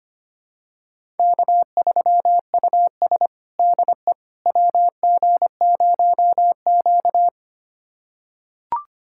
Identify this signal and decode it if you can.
Morse code, heard as a keyed tone, about 25 wpm, 700 Hz. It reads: K3US DE WG0Q
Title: USA Callsign DE Callsign - 4 Characters - 25wpm